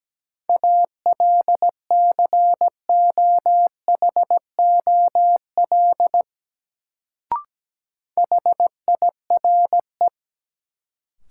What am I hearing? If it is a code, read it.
Morse code, heard as a keyed tone, about 17 wpm, 700 Hz. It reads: ALCOHOL HIRE
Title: Sentences from Top 1400 Words - New Word Drill - 17wpm